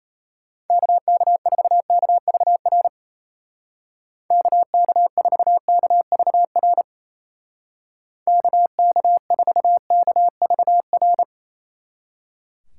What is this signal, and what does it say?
Morse code, heard as a keyed tone, about 32 wpm, 700 Hz. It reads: KK4KVR KK4KVR KK4KVR
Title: Speed Racing - USA Call Signs 28wpm